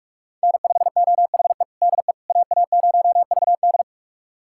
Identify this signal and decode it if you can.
Morse code, heard as a keyed tone, about 46 wpm, 700 Hz. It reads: N5OHE DE AA0VD
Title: USA Callsign DE Callsign - 5 Characters - 45wpm